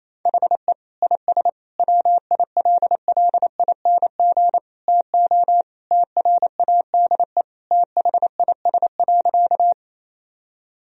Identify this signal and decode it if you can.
Morse code, heard as a keyed tone, about 28 wpm, 700 Hz. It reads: HE IS WILLING TO TRADE THIS.
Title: Sentences from Top 1000 Words - Review Part 4 - 28wpm